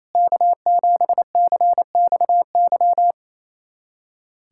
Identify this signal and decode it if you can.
Morse code, heard as a keyed tone, about 28 wpm, 700 Hz. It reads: K7CXY
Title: USA Callsign DE Callsign - 5 Characters - 28wpm